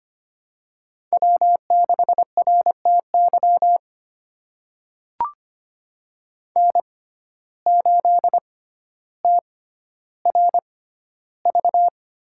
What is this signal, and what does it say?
Morse code, heard as a keyed tone, about 25 wpm, spaced as for 13 wpm, 700 Hz. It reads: W6RTY N8TRV
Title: USA Call Signs - 5 Characters - 6x Character Spacing - 25wpm